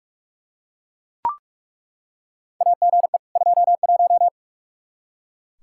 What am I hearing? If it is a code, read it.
Morse code, heard as a keyed tone, about 41 wpm, 700 Hz. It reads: AGE 21
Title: Practice QSO - Exchange 3 - More Info - 45wpm